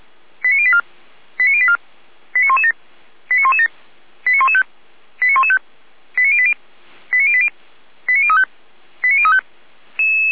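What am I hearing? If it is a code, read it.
Five-tone selective call (ZVEI), heard as quick runs of five tones, tones of about 70 ms. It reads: 89084 89084 89117 89117 89115 89115 89080 89080 89035 89035
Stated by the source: ZVEI five-tone call 89140 received by SDR, Feuerwehr Orlamünde (feuerwehr-orlamuende) fire brigade.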